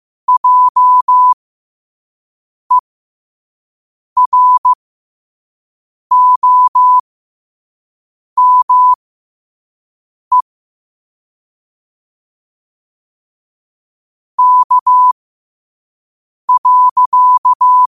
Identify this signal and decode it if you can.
Morse code, heard as a keyed tone, about 15 wpm, spaced as for 5 wpm, 1000 Hz. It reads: JEROME K.
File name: qotdfiles / 5wpm